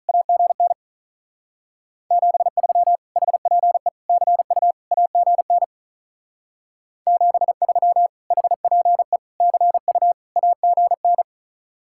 Morse code, heard as a keyed tone, about 45 wpm, 700 Hz. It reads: AGN 73 HPE CU AGN 73 HPE CU AGN